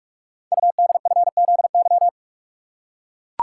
Morse code, heard as a keyed tone, about 44 wpm, 700 Hz. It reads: UDFZY
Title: Code Groups - 5 Letters - 45wpm